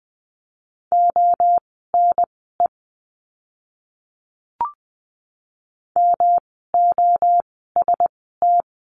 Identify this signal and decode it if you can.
Morse code, heard as a keyed tone, about 20 wpm, spaced as for 15 wpm, 700 Hz. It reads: ONE MOST
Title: 139 Lesson 23f - Letter f - Words 20wpm